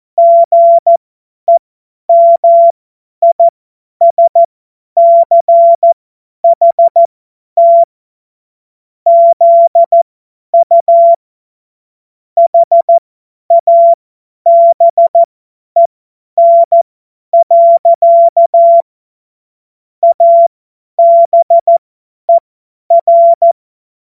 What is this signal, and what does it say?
Morse code, heard as a keyed tone, about 14 wpm, spaced as for 10 wpm, 680 Hz. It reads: GEMISCHT ZU HABEN. ABER
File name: Morsekurs / Witze / Witz 18. Mai 2021, 50 F70